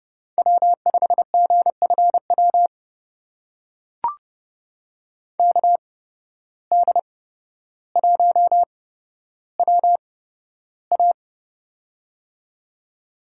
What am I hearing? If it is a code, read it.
Morse code, heard as a keyed tone, about 31 wpm, spaced as for 13 wpm, 700 Hz. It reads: W5GFW KD1WA
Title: USA Call Signs - 5 Characters - 8x Character Spacing - 30wpm